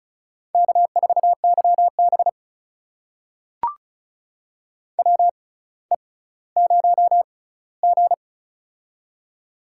Morse code, heard as a keyed tone, about 36 wpm, spaced as for 18 wpm, 700 Hz. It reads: K4YB WE0G
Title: USA Call Signs - 4 Characters - 6x Character Spacing - 35wpm